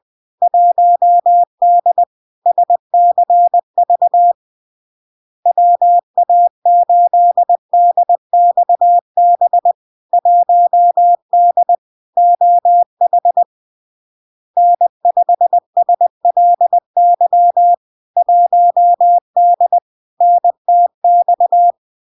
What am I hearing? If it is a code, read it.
Morse code, heard as a keyed tone, about 20 wpm, 700 Hz. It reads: 1D SCV WA8DXB 1D OH N5SLY 1D NTX
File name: LARK-FD-2022-Log-in-CW-20wpm-700-Hz